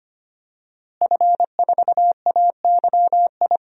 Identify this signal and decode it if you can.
Morse code, heard as a keyed tone, about 25 wpm, 700 Hz. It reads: F4AYI
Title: DXCC-100 Call Signs - 5 Characters - Repeated 2x - 25wpm